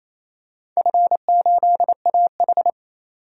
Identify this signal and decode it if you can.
Morse code, heard as a keyed tone, about 28 wpm, 700 Hz. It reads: F8AH
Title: DXCC-100 Call Signs - 4 Characters - 28wpm